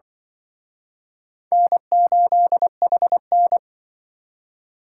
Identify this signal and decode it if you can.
Morse code, heard as a keyed tone, about 25 wpm, 700 Hz. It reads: N8HN
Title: Speed Racing - USA 1x2 Call Signs 20wpm